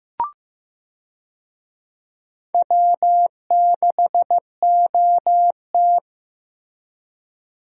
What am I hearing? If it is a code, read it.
Morse code, heard as a keyed tone, about 15 wpm, 700 Hz. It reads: W6OT
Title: USA Call Signs - 4 Characters - Repeated 3x - 15wpm